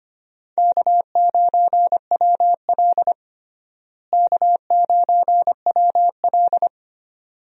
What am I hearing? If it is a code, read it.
Morse code, heard as a keyed tone, about 25 wpm, 700 Hz. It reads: K9WL K9WL
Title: USA Call Signs - 4 Characters - Repeated 2x - 25wpm